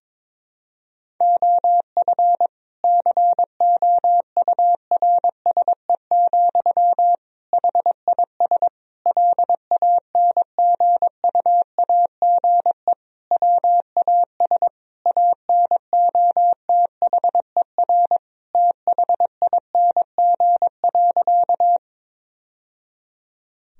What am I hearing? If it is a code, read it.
Morse code, heard as a keyed tone, about 22 wpm, 700 Hz. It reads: OF COURSE, HIS LANGUAGE WAS ANOTHER THING.
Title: Sentences from Top 1000 Words - Review Part 3 - 22wpm